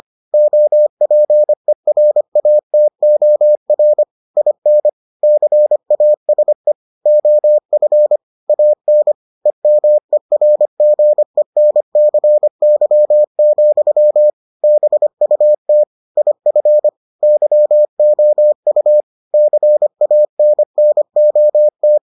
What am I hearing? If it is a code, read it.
Morse code, heard as a keyed tone, about 25 wpm, 600 Hz. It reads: OPERATOR IN CASE OF AN EMERGENCY, BUT IF YOU CANNOT